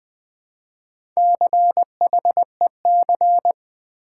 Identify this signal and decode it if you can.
Morse code, heard as a keyed tone, about 20 wpm, 700 Hz. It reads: CHEC
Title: Build A Word - 1 Letter At A Time - 5-Letter Words - 20wpm